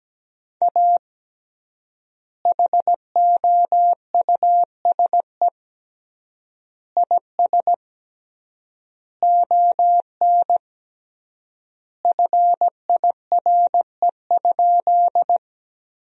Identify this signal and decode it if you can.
Morse code, heard as a keyed tone, about 17 wpm, 700 Hz. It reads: A HOUSE IS ON FIRE?
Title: 3x Word-Spacing - Sentences from Top 400 Words - 17wpm